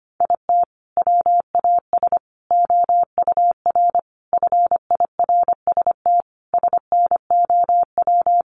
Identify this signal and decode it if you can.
Morse code, heard as a keyed tone, about 25 wpm, 700 Hz. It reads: IT WAS OUR FIRST SNOW
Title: Sentences from Top 1500 Words 25wpm